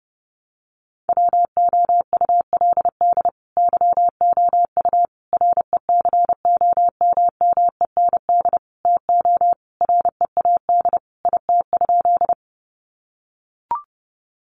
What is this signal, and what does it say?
Morse code, heard as a keyed tone, about 30 wpm, 700 Hz. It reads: WOULD YOU RECOMMEND TO READ IT?
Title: Sentences from Top 1200 Words 30wpm